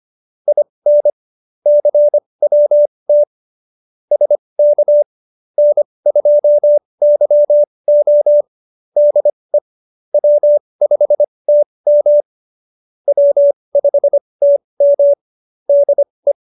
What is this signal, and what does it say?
Morse code, heard as a keyed tone, about 25 wpm, spaced as for 20 wpm, 570 Hz. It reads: IN CWT SK N2YO DE W5TM W5TM DE
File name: QSO316 20